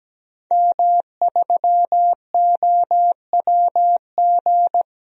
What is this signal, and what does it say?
Morse code, heard as a keyed tone, about 17 wpm, 700 Hz. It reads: M3OWG
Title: DXCC-100 Call Signs - 5 Characters - 17wpm